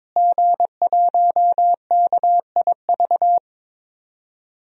Morse code, heard as a keyed tone, about 22 wpm, 700 Hz. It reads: G1KIV
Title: DXCC-100 Call Signs - 5 Characters - Repeated 2x - 22wpm